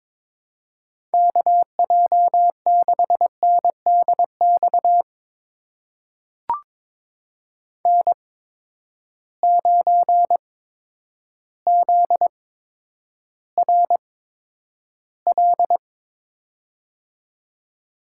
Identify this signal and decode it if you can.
Morse code, heard as a keyed tone, about 21 wpm, spaced as for 10 wpm, 700 Hz. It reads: KJ6NDX N9ZRL
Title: USA Call Signs - 8x Character Spacing - 22wpm